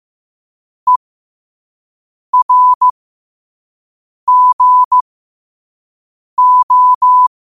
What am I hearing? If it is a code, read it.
Morse code, heard as a keyed tone, about 15 wpm, spaced as for 5 wpm, 1000 Hz. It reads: ERGO
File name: qotdfiles / 5wpm